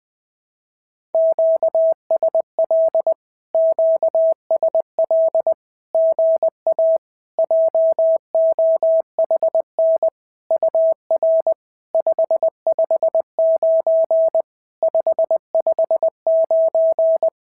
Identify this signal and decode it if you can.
Morse code, heard as a keyed tone, about 20 wpm, 650 Hz. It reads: QSL QSL GA JOHN UR 559 559
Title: SOTA QSO Practice 20wpm